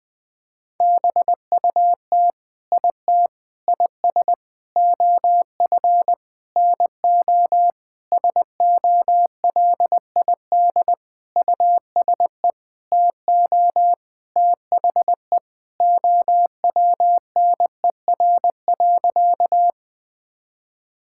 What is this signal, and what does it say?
Morse code, heard as a keyed tone, about 19 wpm, 700 Hz. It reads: BUT IT IS OF NO SOLID USE TO THE OWNER.
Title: Sentences from Top 1600 Words 20wpm